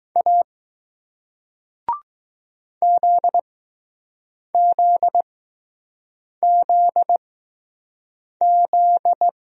Morse code, heard as a keyed tone, about 20 wpm, spaced as for 6 wpm, 700 Hz. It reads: A ZZZZ